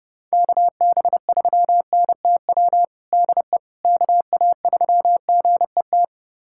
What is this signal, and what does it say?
Morse code, heard as a keyed tone, about 30 wpm, 700 Hz. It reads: KB3NTW DE KA3GET
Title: USA Callsign DE Callsign - 6 Characters - 30wpm